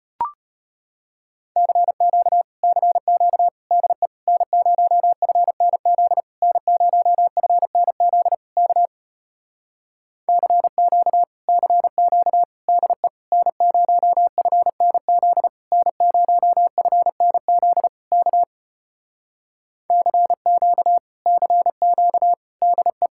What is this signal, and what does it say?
Morse code, heard as a keyed tone, about 33 wpm, 700 Hz. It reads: CQ CQ DE N0FNZ N0FNZ K CQ CQ DE N0FNZ N0FNZ K CQ CQ DE